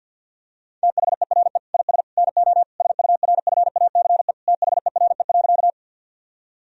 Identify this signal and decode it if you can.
Morse code, heard as a keyed tone, about 50 wpm, 700 Hz. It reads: THERE IS NO SURFACE THERE.
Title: Sentences from Top 1000 Words - Review Part 3 - 50wpm